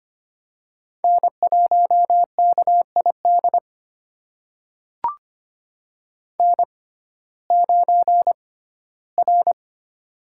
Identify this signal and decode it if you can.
Morse code, heard as a keyed tone, about 25 wpm, spaced as for 15 wpm, 700 Hz. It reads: N1KID N9R
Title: USA Call Signs - 6x Character Spacing - 25wpm